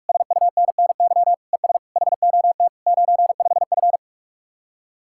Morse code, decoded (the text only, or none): SUNNY ES HOT 95F